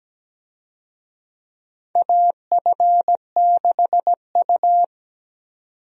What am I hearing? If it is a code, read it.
Morse code, heard as a keyed tone, about 17 wpm, 700 Hz. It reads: AF6U
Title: USA Call Signs - 4 Characters - 17wpm